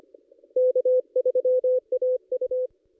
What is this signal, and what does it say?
Morse code, heard as a keyed tone, about 25 wpm, 510 Hz. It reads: K3AU
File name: contests / ss-cw-k5tr-op-n5rz -1800